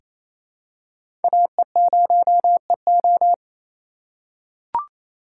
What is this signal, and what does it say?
Morse code, heard as a keyed tone, about 27 wpm, 700 Hz. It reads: AE0EO
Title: USA Call Signs - 5 Characters - 2x Character Spacing - 28wpm